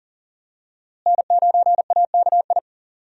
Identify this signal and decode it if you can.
Morse code, heard as a keyed tone, about 40 wpm, 700 Hz. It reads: N9AKI
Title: USA Call Signs - 5 Characters - Repeated 3x - 40wpm